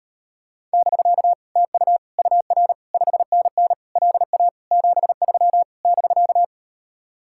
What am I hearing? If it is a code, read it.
Morse code, heard as a keyed tone, about 38 wpm, 700 Hz. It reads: <BK> TU UR 5NN LA 73 <BK>